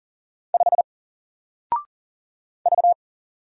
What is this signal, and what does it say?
Morse code, heard as a keyed tone, about 40 wpm, 700 Hz. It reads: 5 V
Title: Single Letter-Number 40wpm